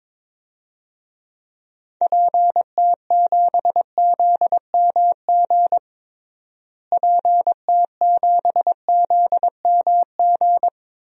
Morse code, heard as a keyed tone, about 21 wpm, 700 Hz. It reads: PT7ZMG PT7ZMG
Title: DXCC-100 Call Signs - 6 Characters - Repeated 2x - 22wpm